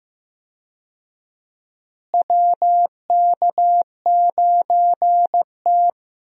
Morse code, heard as a keyed tone, about 15 wpm, 700 Hz. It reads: WK9T
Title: USA Call Signs - 4 Characters - Repeated 2x - 15wpm